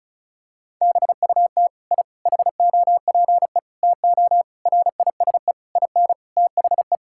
Morse code, heard as a keyed tone, about 34 wpm, 700 Hz. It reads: BUT I HOPE TO RISE IN THE
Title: Sentences from Top 1000 Words - Review Part 6 - 35wpm